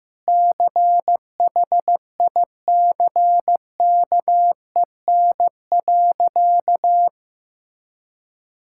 Morse code, heard as a keyed tone, about 16 wpm, 700 Hz. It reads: CHICKEN.
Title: Sentences from Top 1000 Words - Review Part 5 - 15wpm